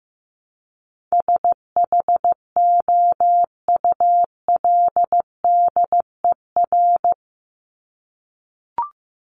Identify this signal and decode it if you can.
Morse code, heard as a keyed tone, about 15 wpm, 700 Hz. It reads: SHOULDER
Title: Top 1501-1750 Words 15wpm